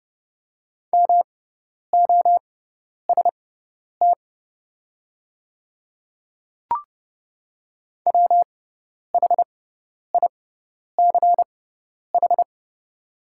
Morse code, heard as a keyed tone, about 30 wpm, spaced as for 10 wpm, 700 Hz. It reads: MOST WHICH
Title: 6x Character Spacing - Top 100 Words - 30wpm